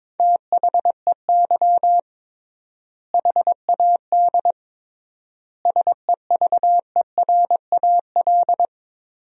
Morse code, heard as a keyed tone, about 22 wpm, 700 Hz. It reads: THEY HAD SEVERAL
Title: 3x Word-Spacing - Sentences from Top 400 Words - 22wpm